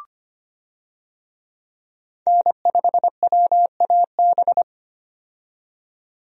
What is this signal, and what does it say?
Morse code, heard as a keyed tone, about 25 wpm, 700 Hz. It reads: N5WAB
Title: USA Call Signs - 5 Characters - 25wpm